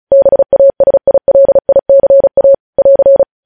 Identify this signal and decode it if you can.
Morse code, heard as a keyed tone, about 36 wpm, 550 Hz. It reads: BASILICA <AR>
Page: news headlines in morse code at 35 wpm